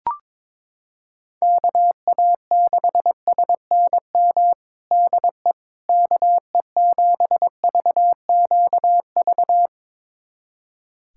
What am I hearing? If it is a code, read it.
Morse code, heard as a keyed tone, about 21 wpm, 700 Hz. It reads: KA6SNM DE KE7VQV
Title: USA Callsign DE Callsign - 6 Characters - 22wpm